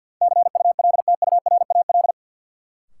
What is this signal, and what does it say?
Morse code, heard as a keyed tone, about 53 wpm, 700 Hz. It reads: CULTURAL
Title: Sentences from Top 1500 Words - New Word Drill - 50wpm